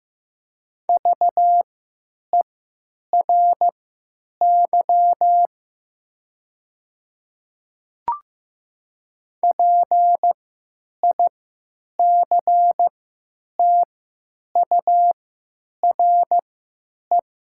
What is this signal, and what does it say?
Morse code, heard as a keyed tone, about 15 wpm, spaced as for 9 wpm, 690 Hz. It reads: VERY PICTURE
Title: 3x Character Spacing - Top 300 Words - 15wpm